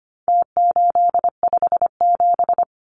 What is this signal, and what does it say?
Morse code, heard as a keyed tone, about 25 wpm, 700 Hz. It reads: T857
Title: Numbers - Contest Style - 4-Digit 25wpm